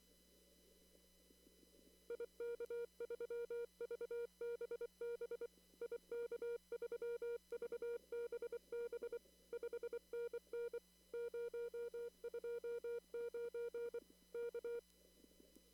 Morse code, heard as a keyed tone, about 24 wpm, 470 Hz. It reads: IK3VBB IK3VBB 5NN 029 K
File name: CW / kt5j wpx cw-op-k5pi